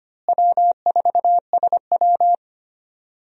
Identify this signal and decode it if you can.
Morse code, heard as a keyed tone, about 25 wpm, 700 Hz. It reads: W4SW
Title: USA Call Signs - 4 Characters - Repeated 3x - 25wpm